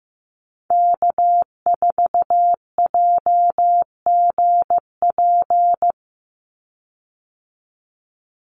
Morse code, heard as a keyed tone, about 15 wpm, 700 Hz. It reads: K4JGP
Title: USA Call Signs 15wpm